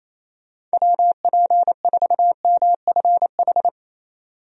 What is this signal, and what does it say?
Morse code, heard as a keyed tone, about 28 wpm, 700 Hz. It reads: WP4MFH
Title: USA Call Signs - 6 Characters - 28wpm